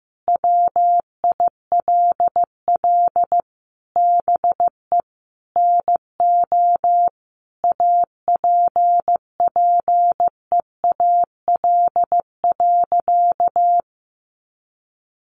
Morse code, heard as a keyed tone, about 15 wpm, 700 Hz. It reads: WILL BE NO APPEAL.
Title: Sentences from Top 1300 Words 15wpm